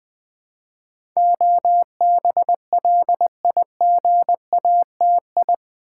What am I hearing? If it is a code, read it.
Morse code, heard as a keyed tone, about 20 wpm, 700 Hz. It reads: OBLIGATI
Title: Build A Word - 4 Letters At A Time - 10-Letter Words - 20wpm